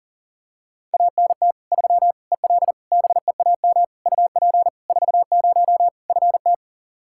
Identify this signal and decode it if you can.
Morse code, heard as a keyed tone, about 40 wpm, 700 Hz. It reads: ANT 3 EL BEAM UP 40 FT
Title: K1USN SST QSO Practice 40wpm